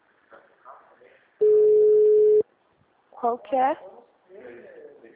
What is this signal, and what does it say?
Ringback tone (Europe, 425 Hz), with speech.